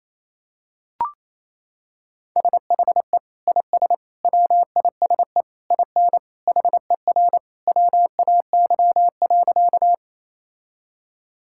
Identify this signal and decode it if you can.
Morse code, heard as a keyed tone, about 28 wpm, 700 Hz. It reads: SHE IS WISE IN HER WAY.
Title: Sentences from Top 1600 Words 28wpm